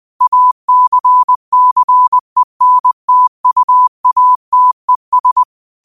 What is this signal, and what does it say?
Morse code, heard as a keyed tone, about 21 wpm, 1000 Hz. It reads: ACCENTUATES
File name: 20wpm